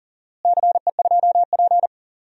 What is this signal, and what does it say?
Morse code, heard as a keyed tone, about 38 wpm, 700 Hz. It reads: CE2P
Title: DXCC-100 Call Signs - 4 Characters - Repeated 3x - 40wpm